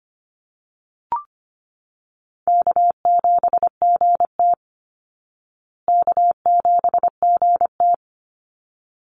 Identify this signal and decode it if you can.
Morse code, heard as a keyed tone, about 25 wpm, 700 Hz. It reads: K7GT K7GT
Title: Easy Call Signs Repeated 2x - 25wpm